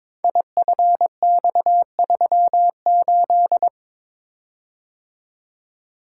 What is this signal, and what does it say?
Morse code, heard as a keyed tone, about 22 wpm, 700 Hz. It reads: IFX38
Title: Code Groups - 5 Letter:Numbers - 22wpm